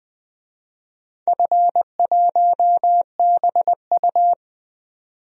Morse code, heard as a keyed tone, about 20 wpm, 700 Hz. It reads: F1BU
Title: DXCC-100 Call Signs - 4 Characters - 20wpm